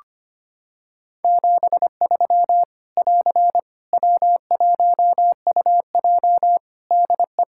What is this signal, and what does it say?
Morse code, heard as a keyed tone, about 25 wpm, 700 Hz. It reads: 73 <AR> W1UJ DE
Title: Practice QSO - Exchange 4 - Ending - 25wpm